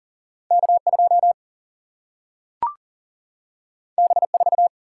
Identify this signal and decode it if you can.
Morse code, heard as a keyed tone, about 40 wpm, 700 Hz. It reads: K 2 B 4
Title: Single-Letter Single-Number 40wpm